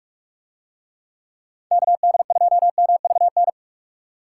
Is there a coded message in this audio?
Yes: Morse code from 1.7 to 3.5 s, heard as a keyed tone, about 44 wpm, 700 Hz. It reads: KD2MVN